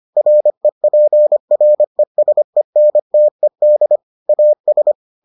Morse code, heard as a keyed tone, about 25 wpm, 600 Hz. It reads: REPRESENTED AS